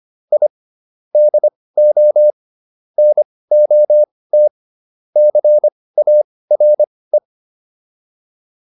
Morse code, heard as a keyed tone, about 25 wpm, spaced as for 18 wpm, 600 Hz. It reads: I DO NOT CARE